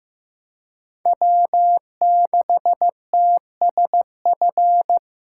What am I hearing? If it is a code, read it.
Morse code, heard as a keyed tone, about 18 wpm, 700 Hz. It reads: W6TSF
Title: Speed Racing - USA Call Signs 15wpm